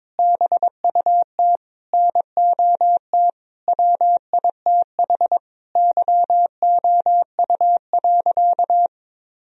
Morse code, heard as a keyed tone, about 22 wpm, 700 Hz. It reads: BUT NOT WITH YOU.
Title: Sentences from Top 1000 Words - Review Part 4 - 22wpm